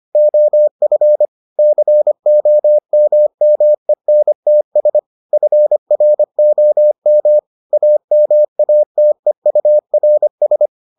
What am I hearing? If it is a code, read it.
Morse code, heard as a keyed tone, about 25 wpm, 600 Hz. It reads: OF COMMENTS FROM AMATEURS